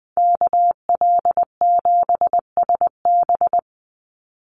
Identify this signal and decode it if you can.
Morse code, heard as a keyed tone, about 20 wpm, 700 Hz. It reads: KL7SB
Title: Call Signs Repeated 2x - 20wpm